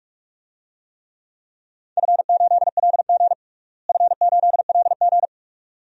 Morse code, heard as a keyed tone, about 46 wpm, 700 Hz. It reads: F8LG F8LG
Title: DXCC-100 Call Signs - 4 Characters - Repeated 2x - 45wpm